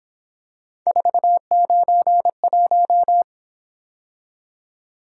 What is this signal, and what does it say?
Morse code, heard as a keyed tone, about 27 wpm, 700 Hz. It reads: 491
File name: Speed-Racing - Three Letter-Number Combinations - 22wpm